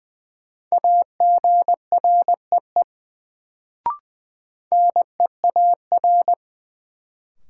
Morse code, heard as a keyed tone, about 20 wpm, 700 Hz. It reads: AGREE NEAR
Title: Sets of 5 Words Encore - New Word Drill - 20wpm